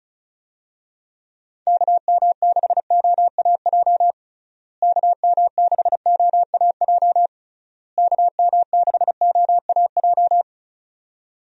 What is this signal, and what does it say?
Morse code, heard as a keyed tone, about 34 wpm, 700 Hz. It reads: KM6OAJ KM6OAJ KM6OAJ
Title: USA Call Signs - 6 Characters - Repeated 3x - 35wpm